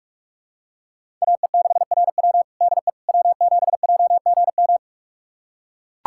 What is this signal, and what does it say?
Morse code, heard as a keyed tone, about 41 wpm, 700 Hz. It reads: AE6RW DE W7JGM
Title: USA Callsign DE Callsign - 5 Characters - 45wpm